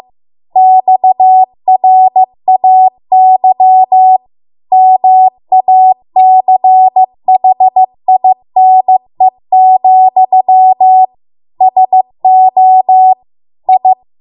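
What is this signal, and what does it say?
Morse code, heard as a keyed tone, about 15 wpm, 750 Hz. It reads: XRAY MACHINE, SO I
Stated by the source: Morse code practice at 15 wpm from the national amateur radio association's code archive.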